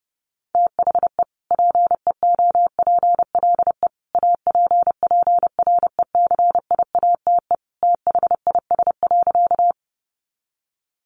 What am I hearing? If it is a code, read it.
Morse code, heard as a keyed tone, about 30 wpm, 700 Hz. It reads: THE PEOPLE APPRECIATE THIS.
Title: Sentences from Top 1300 Words 30wpm